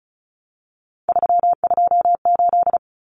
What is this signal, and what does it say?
Morse code, heard as a keyed tone, about 34 wpm, 700 Hz. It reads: 328
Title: Common Cut Numbers - 3-Digit 35wpm